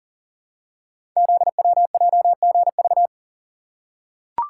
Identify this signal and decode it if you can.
Morse code, heard as a keyed tone, about 38 wpm, 690 Hz. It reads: ZWJGV
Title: Code Groups - 5 Letters - 40wpm